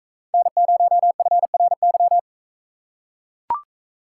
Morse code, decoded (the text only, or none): N0FRY